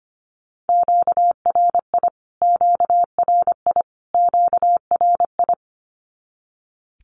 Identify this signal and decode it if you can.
Morse code, heard as a keyed tone, about 25 wpm, 700 Hz. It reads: QRI QRI QRI
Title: Q-Codes Repeated 3x 25wpm